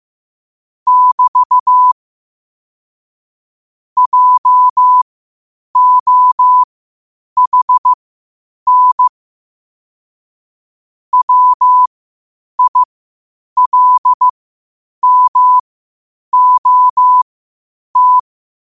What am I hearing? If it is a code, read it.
Morse code, heard as a keyed tone, about 15 wpm, spaced as for 8 wpm, 1000 Hz. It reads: <BT> JOHN WILMOT
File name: qotdfiles / 8wpm